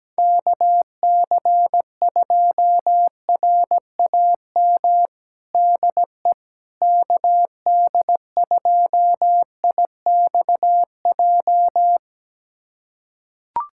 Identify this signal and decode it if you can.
Morse code, heard as a keyed tone, about 17 wpm, 700 Hz. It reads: KC2RAM DE KD2IXJ
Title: USA Callsign DE Callsign - 6 Characters - 17wpm